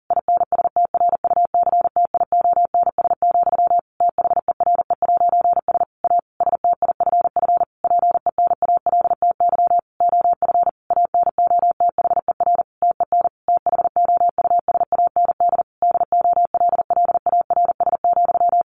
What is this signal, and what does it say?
Morse code, heard as a keyed tone, about 40 wpm, 700 Hz. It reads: INSTRUCTIONS, THERE'S A STIFF PENALTY OF ANOTHER TEN THOUSAND DOLLARS,
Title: A Sound Of Thunder 40wpm